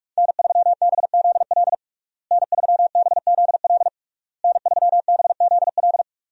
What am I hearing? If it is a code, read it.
Morse code, heard as a keyed tone, about 41 wpm, 700 Hz. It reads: N3BZL N3BZL N3BZL
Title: USA Call Signs - 5 Characters - Repeated 3x - 45wpm